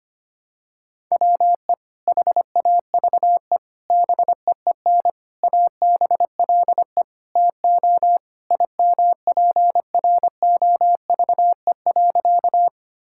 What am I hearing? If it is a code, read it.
Morse code, heard as a keyed tone, about 25 wpm, 700 Hz. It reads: WE HAVE BEEN ABLE TO IMPROVE.
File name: practice / 25wpm / Sentences with 4-Word Phrases - 25wpm